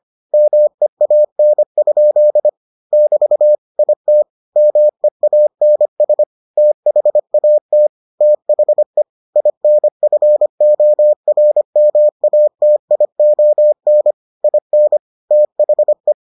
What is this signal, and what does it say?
Morse code, heard as a keyed tone, about 25 wpm, 600 Hz. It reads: MEAN? <BT> IT MEANS THAT THE INFORMATION IN THE